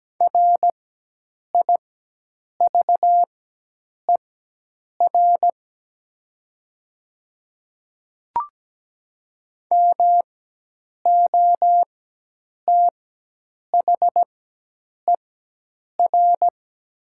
Morse code, heard as a keyed tone, about 17 wpm, spaced as for 8 wpm, 700 Hz. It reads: RIVER MOTHER
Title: 4x Character Spacing - Top 300 Words - 17wpm